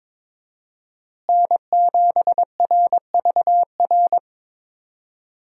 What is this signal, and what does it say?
Morse code, heard as a keyed tone, about 22 wpm, 700 Hz. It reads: N7RVR
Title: USA Call Signs - 5 Characters - Repeated 2x - 22wpm